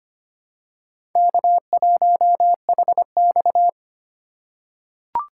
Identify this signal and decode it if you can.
Morse code, heard as a keyed tone, about 25 wpm, 700 Hz. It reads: K1HX
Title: USA Call Signs - 4 Characters - 3x Character Spacing - 25wpm